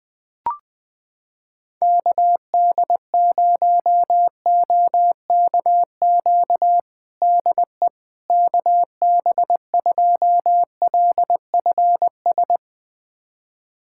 Morse code, heard as a keyed tone, about 20 wpm, 700 Hz. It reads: KD0OKQ DE KB2LFS
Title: USA Callsign DE Callsign - 6 Characters - 20wpm